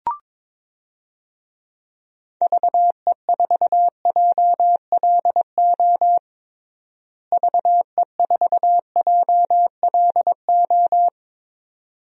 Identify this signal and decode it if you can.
Morse code, heard as a keyed tone, about 22 wpm, 700 Hz. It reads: VE4JLO VE4JLO
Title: DXCC-100 Call Signs - 6 Characters - Repeated 2x - 22wpm